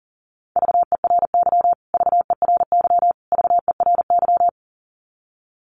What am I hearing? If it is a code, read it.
Morse code, heard as a keyed tone, about 40 wpm, 700 Hz. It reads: VERY VERY VERY
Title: Top 300 Words Repeated 40wpm